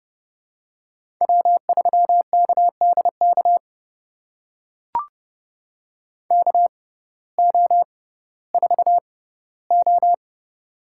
Morse code, heard as a keyed tone, about 30 wpm, spaced as for 17 wpm, 700 Hz. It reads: W3KDK KO4O
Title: USA Call Signs - 5 Characters - 6x Character Spacing - 30wpm